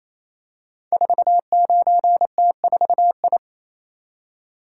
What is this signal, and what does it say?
Morse code, heard as a keyed tone, about 29 wpm, 690 Hz. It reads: 49T4I
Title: Code Groups - 5 Letter:Numbers - 28wpm